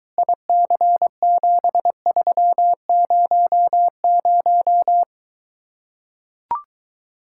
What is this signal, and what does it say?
Morse code, heard as a keyed tone, about 24 wpm, 700 Hz. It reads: IC7300